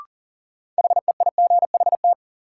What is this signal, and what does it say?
Morse code, heard as a keyed tone, about 39 wpm, 700 Hz. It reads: HEIGHT
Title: Sentences from Top 1600 Words - New Word Drill Alt - 40wpm